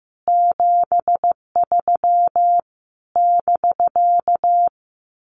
Morse code, heard as a keyed tone, about 15 wpm, 700 Hz. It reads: 73 <BK>
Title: Common QSO Phrases Encore 15wpm